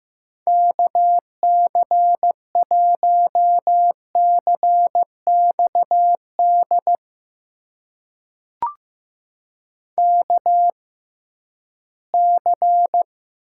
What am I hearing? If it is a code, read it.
Morse code, heard as a keyed tone, about 15 wpm, spaced as for 11 wpm, 700 Hz. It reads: KC1CXD KC
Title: USA Call Signs - 6 Characters - 6x Character Spacing - 15wpm